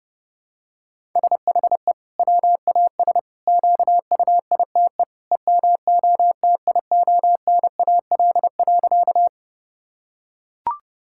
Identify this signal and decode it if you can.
Morse code, heard as a keyed tone, about 30 wpm, 700 Hz. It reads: SHE WAS QUITE EMOTIONAL.
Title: Sentences from Top 1000 Words - Review Part 5 - 30wpm